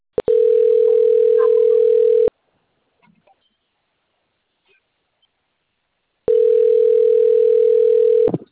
Ringback tone (North America), 440 + 480 Hz.